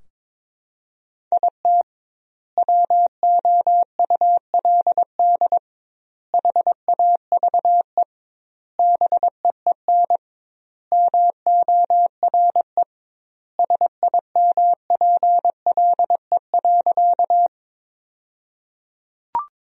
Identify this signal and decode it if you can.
Morse code, heard as a keyed tone, about 22 wpm, 700 Hz. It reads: IT WOULD HAVE BEEN MORE SIMPLE.
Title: 2x Word-Spacing - Sentences from Top 400 Words - 22wpm